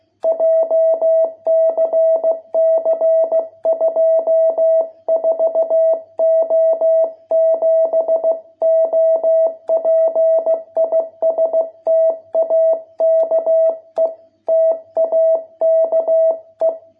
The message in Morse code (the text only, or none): JCC 24O7 OP IS TAKE TAKE